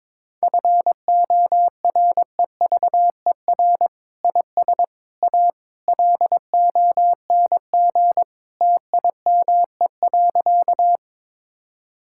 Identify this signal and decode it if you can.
Morse code, heard as a keyed tone, about 22 wpm, 700 Hz. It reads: FOREVER IS A LONG TIME.
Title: Sentences from Top 1600 Words 22wpm